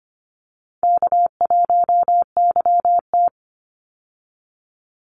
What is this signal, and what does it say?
Morse code, heard as a keyed tone, about 25 wpm, 700 Hz. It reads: K1YT
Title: Easy Call Signs Repeated 2x - 25wpm